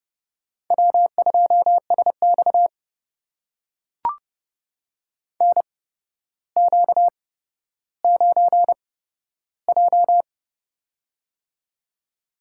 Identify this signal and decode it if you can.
Morse code, heard as a keyed tone, about 30 wpm, spaced as for 13 wpm, 700 Hz. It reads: W2SX NQ9J
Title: USA Call Signs - 8x Character Spacing - 30wpm